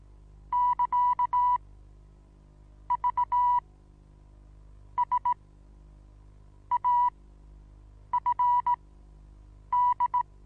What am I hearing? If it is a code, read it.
Morse code, heard as a keyed tone, about 17 wpm, spaced as for 7 wpm, 980 Hz. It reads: <KA> V S AFD
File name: Onlinelehrgang / morsen / DJ4UF Morsen25